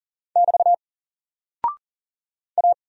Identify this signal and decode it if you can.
Morse code, heard as a keyed tone, about 40 wpm, 700 Hz. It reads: <BT> A
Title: Single Character (Letter:Number:Punctuation:Prosign) Alt 40wpm